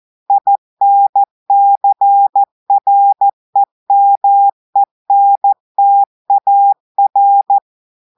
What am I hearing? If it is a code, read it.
Morse code, heard as a keyed tone, about 14 wpm, 800 Hz. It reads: INCREMENTAR
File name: audios / portugais / portugais texte en clair 2 vitesse 14 mots minutes